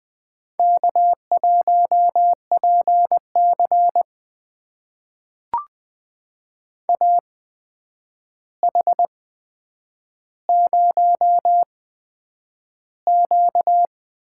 Morse code, heard as a keyed tone, about 21 wpm, spaced as for 9 wpm, 700 Hz. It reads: K1PC AH0Q
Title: USA Call Signs - 8x Character Spacing - 20wpm